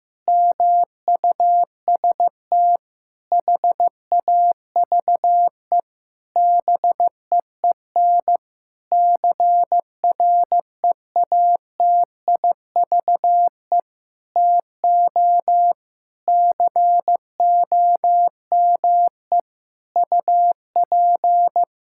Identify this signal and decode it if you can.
Morse code, heard as a keyed tone, about 15 wpm, 700 Hz. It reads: MUST HAVE BEEN CREATIVE TO COME UP
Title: Sentences from Top 1000 Words - Review Part 5 - 15wpm